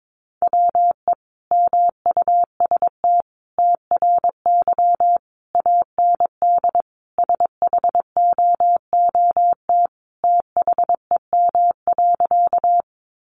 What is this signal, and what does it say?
Morse code, heard as a keyed tone, about 22 wpm, 700 Hz. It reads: WE MUST TRY AND SHOOT THEM.
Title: Sentences from Top 1200 Words 22wpm